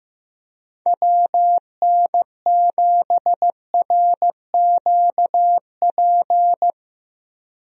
Morse code, heard as a keyed tone, about 15 wpm, 700 Hz. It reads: WN7RQP
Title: USA Call Signs - 6 Characters - Repeated 2x - 15wpm